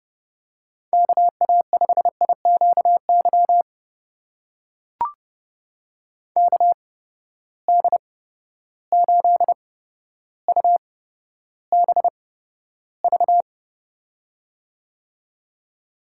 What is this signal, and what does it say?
Morse code, heard as a keyed tone, about 31 wpm, spaced as for 13 wpm, 700 Hz. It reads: KA5IQY KD8UBV
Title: USA Call Signs - 6 Characters - 8x Character Spacing - 30wpm